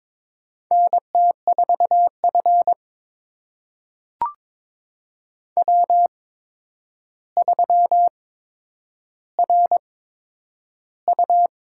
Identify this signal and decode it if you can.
Morse code, heard as a keyed tone, about 21 wpm, spaced as for 9 wpm, 700 Hz. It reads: NT4F W3RU
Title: USA Call Signs - 4 Characters - 8x Character Spacing - 22wpm